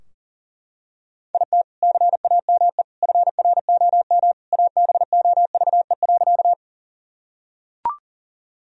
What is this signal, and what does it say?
Morse code, heard as a keyed tone, about 40 wpm, 700 Hz. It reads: IT CAME FROM ABOVE.
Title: Sentences from Top 1000 Words - Review Part 2 - 40wpm